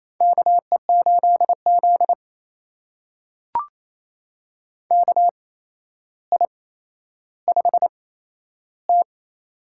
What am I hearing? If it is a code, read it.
Morse code, heard as a keyed tone, about 28 wpm, spaced as for 12 wpm, 700 Hz. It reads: KE8Z KI5T
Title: USA Call Signs - 8x Character Spacing - 28wpm